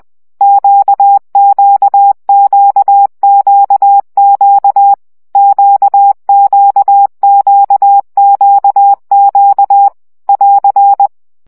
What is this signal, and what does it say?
Morse code, heard as a keyed tone, about 20 wpm, 800 Hz. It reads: QQQQQ QQQQQ <AR>
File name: CW gyakorlas q 17 baud